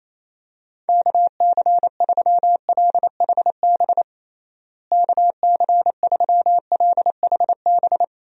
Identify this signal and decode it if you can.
Morse code, heard as a keyed tone, about 27 wpm, 700 Hz. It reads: KC3LHB KC3LHB
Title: USA Call Signs - 6 Characters - Repeated 3x - 28wpm